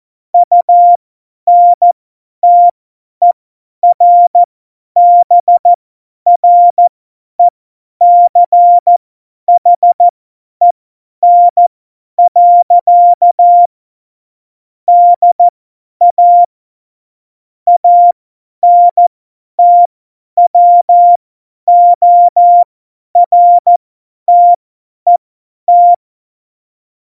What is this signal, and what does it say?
Morse code, heard as a keyed tone, about 14 wpm, spaced as for 10 wpm, 700 Hz. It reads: UNTERBRECHEN. DA ANTWORTET
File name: Morsekurs / Witze / Witz 22. Oktober 2019, 50 F70